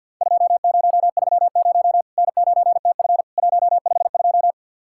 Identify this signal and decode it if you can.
Morse code, heard as a keyed tone, about 50 wpm, 700 Hz. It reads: 2030 N9TF 152
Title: Worked All Europe DX Contest - QTC Exchange with Repeats - 50wpm